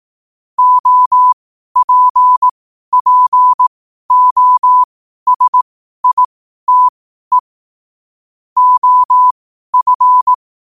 Morse code, heard as a keyed tone, about 18 wpm, spaced as for 12 wpm, 1000 Hz. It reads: OPPOSITE OF
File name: qotdfiles / 12wpm